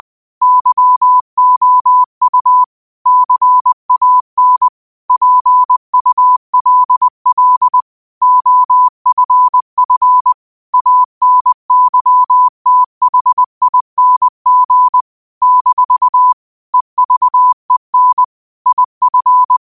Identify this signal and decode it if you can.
Morse code, heard as a keyed tone, about 20 wpm, 1000 Hz. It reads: YOU CAN PULL OFF ANYTHING - EVEN IF